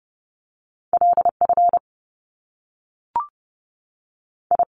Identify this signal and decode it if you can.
Morse code, heard as a keyed tone, about 30 wpm, 700 Hz. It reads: L F I
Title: Top Two Letter Combinations in Words 30wpm